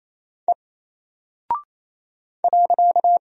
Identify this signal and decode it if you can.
Morse code, heard as a keyed tone, about 28 wpm, 700 Hz. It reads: E .
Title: Single Character (Letter:Number:Punctuation:Prosign) 28wpm